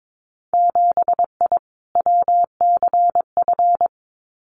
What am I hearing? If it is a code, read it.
Morse code, heard as a keyed tone, about 22 wpm, 700 Hz. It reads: 7I WCF
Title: Winter Field Day Exchanges Encore 22wpm